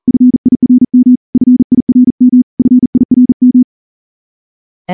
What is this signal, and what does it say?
Morse code, heard as a keyed tone, about 38 wpm, 250 Hz. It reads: FIRM FIRM FIRM
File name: morse-training-40WPM